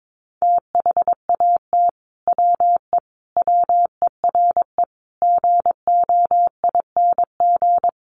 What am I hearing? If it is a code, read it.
Morse code, heard as a keyed tone, about 22 wpm, 700 Hz. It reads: THAT WE WERE GOING
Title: Sets of 5 Words Encore2 22wpm